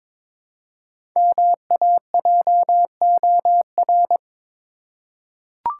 Morse code, heard as a keyed tone, about 22 wpm, 700 Hz. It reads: MAJOR
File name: 5 Letter Words - First 3 Letters Given - 22wpm